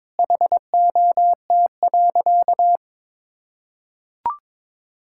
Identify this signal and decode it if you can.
Morse code, heard as a keyed tone, about 22 wpm, 700 Hz. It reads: HOT.
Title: Sentences from Top 1000 Words - Review Part 6 - 22wpm